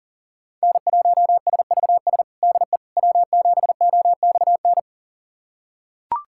Morse code, heard as a keyed tone, about 37 wpm, 700 Hz. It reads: N1SVS DE W7OXN
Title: USA Callsign DE Callsign - 5 Characters - 40wpm